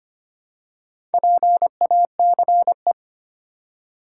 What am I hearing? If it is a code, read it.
Morse code, heard as a keyed tone, about 25 wpm, 700 Hz. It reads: PACE
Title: Four Letter Words - 25wpm